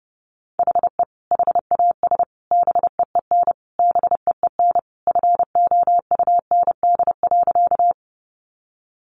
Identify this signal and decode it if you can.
Morse code, heard as a keyed tone, about 30 wpm, 700 Hz. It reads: HE HAS BEEN BEEN FOUND.
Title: Sentences from Top 300 Words 30wpm